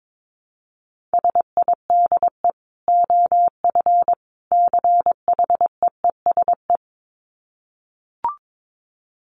Morse code, heard as a keyed tone, about 22 wpm, 700 Hz. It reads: SIDE OF CHEESE
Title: Dit Challenge 22wpm